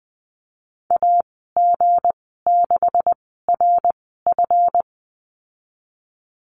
Morse code, heard as a keyed tone, about 20 wpm, spaced as for 15 wpm, 700 Hz. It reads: AG6RF
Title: 179 Lesson 29g - Slash - Call Signs 20wpm